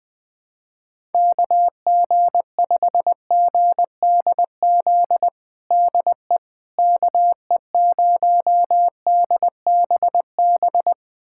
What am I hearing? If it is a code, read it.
Morse code, heard as a keyed tone, about 20 wpm, 700 Hz. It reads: KG5GDZ DE KE0DBB
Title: USA Callsign DE Callsign - 6 Characters - 20wpm